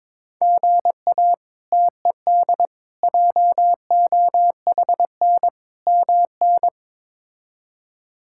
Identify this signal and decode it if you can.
Morse code, heard as a keyed tone, about 22 wpm, 700 Hz. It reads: GA TED JOHN MN
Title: K1USN SST QSO Practice - 22wpm